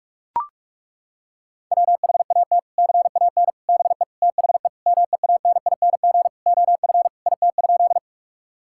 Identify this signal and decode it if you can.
Morse code, heard as a keyed tone, about 41 wpm, 700 Hz. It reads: WHAT CAN BE THE MEANING OF IT?
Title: Sentences from Top 1000 Words - Review Part 6 - 45wpm